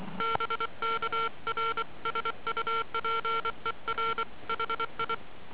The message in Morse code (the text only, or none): BK R SUPER HI